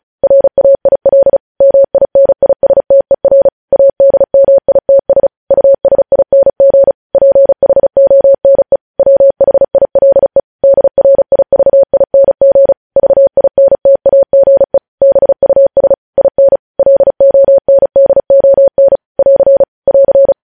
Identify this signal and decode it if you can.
Morse code, heard as a keyed tone, about 36 wpm, 550 Hz. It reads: RAIL MINISTER ADMITS USING PHONE WHILE DRIVING VINTAGE BUS IN LONDON <AR> <AR>
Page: News Headlines in Morse Code at 35 WPM